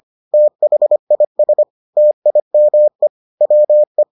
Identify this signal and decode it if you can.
Morse code, heard as a keyed tone, about 25 wpm, 600 Hz. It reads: THIS TIME WE